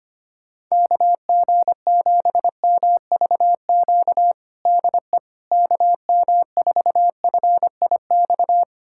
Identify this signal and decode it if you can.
Morse code, heard as a keyed tone, about 25 wpm, 700 Hz. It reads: KG7MVQ DE KM4FIX
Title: USA Callsign DE Callsign 25wpm